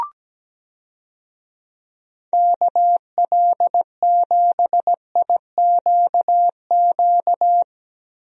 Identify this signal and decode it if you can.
Morse code, heard as a keyed tone, about 17 wpm, 700 Hz. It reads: KL7IQQ
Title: DXCC-100 Call Signs - 6 Characters - 17wpm